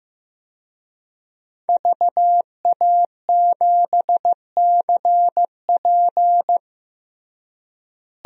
Morse code, heard as a keyed tone, about 15 wpm, 700 Hz. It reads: VA7CP
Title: DXCC-100 Call Signs - 5 Characters - Repeated 3x - 15wpm